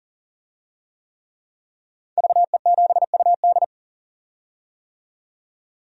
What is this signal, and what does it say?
Morse code, heard as a keyed tone, about 40 wpm, 700 Hz. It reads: VE7UD
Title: DXCC-100 Call Signs - 5 Characters - 40wpm